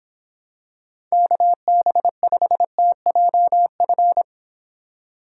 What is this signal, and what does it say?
Morse code, heard as a keyed tone, about 26 wpm, 700 Hz. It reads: KB5TJF